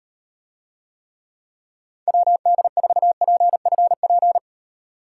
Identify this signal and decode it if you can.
Morse code, heard as a keyed tone, about 38 wpm, 700 Hz. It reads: WD4PFP